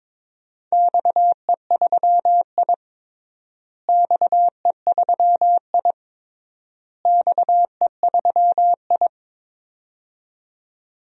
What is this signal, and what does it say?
Morse code, heard as a keyed tone, about 21 wpm, 700 Hz. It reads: XE3I XE3I XE3I
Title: DXCC-100 Call Signs - 4 Characters - Repeated 3x - 22wpm